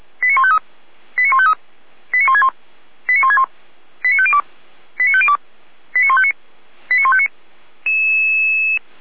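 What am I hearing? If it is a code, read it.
Five-tone selective call (ZVEI), heard as quick runs of five tones, tones of about 70 ms. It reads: 89153 89153 89161 89161 89502 89502 89159 89159
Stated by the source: ZVEI five-tone call 89160 received by SDR, fire brigade Orlamünde (feuerwehr-orlamuende).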